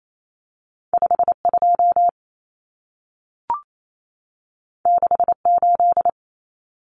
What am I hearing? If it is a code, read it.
Morse code, heard as a keyed tone, about 28 wpm, 700 Hz. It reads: 52 68
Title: Common Cut Numbers - 2-Digit 28wpm